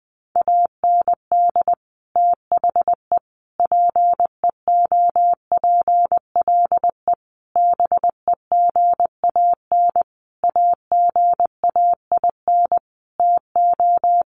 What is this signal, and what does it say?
Morse code, heard as a keyed tone, about 20 wpm, 700 Hz. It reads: AND THE PEOPLE BEGAN AGAIN TO
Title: Sentences from Top 1500 Words 20wpm